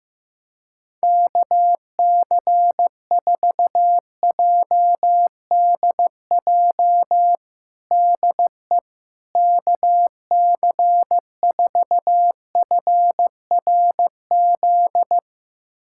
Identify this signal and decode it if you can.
Morse code, heard as a keyed tone, about 15 wpm, 700 Hz. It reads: KC4JDJ DE KC4FRZ
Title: USA Callsign DE Callsign - 6 Characters - 15wpm